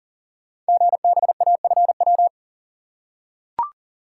Morse code, heard as a keyed tone, about 42 wpm, 690 Hz. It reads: GBAFW
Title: Code Groups - 5 Letters - 40wpm